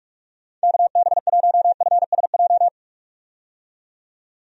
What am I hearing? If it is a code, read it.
Morse code, heard as a keyed tone, about 46 wpm, 700 Hz. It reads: KB1FSJ
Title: USA Call Signs - 6 Characters - 45wpm